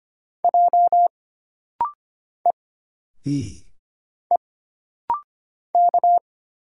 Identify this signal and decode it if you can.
Morse code, heard as a keyed tone, about 25 wpm, 700 Hz. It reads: J E E K
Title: Single Letters - Rapid-Fire 25wpm